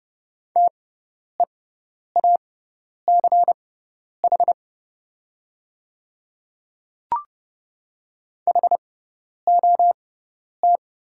Morse code, heard as a keyed tone, about 30 wpm, spaced as for 10 wpm, 700 Hz. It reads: TEACH HOT
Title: 6x Character Spacing - Top 500 Words - 30wpm